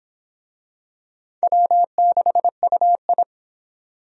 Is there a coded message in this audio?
Yes: Morse code from 1.4 to 3.2 s, heard as a keyed tone, about 25 wpm, 700 Hz. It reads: W6UI